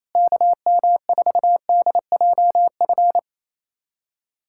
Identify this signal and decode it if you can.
Morse code, heard as a keyed tone, about 28 wpm, 700 Hz. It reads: KM4DJF
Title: USA Callsign DE Callsign 28wpm